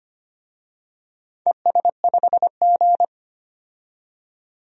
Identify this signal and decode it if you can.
Morse code, heard as a keyed tone, about 25 wpm, 700 Hz. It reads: ES5G
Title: DXCC-100 Call Signs - 4 Characters - 25wpm